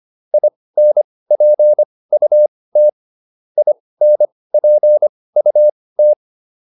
Morse code, heard as a keyed tone, about 25 wpm, spaced as for 18 wpm, 600 Hz. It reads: INPUT INPUT